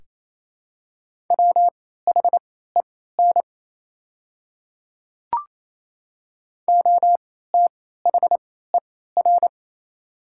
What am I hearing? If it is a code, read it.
Morse code, heard as a keyed tone, about 28 wpm, spaced as for 16 wpm, 700 Hz. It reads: WHEN OTHER
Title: 3x Character Spacing - Top 300 Words - 28wpm